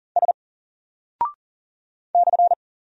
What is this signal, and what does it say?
Morse code, heard as a keyed tone, about 40 wpm, 700 Hz. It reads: S /
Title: Single Character (Letter:Number:Punctuation:Prosign) 40wpm